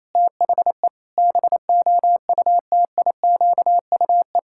Morse code, heard as a keyed tone, about 28 wpm, 700 Hz. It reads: THE BOUTIQUE
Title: Pangram Challenge v2 - 28wpm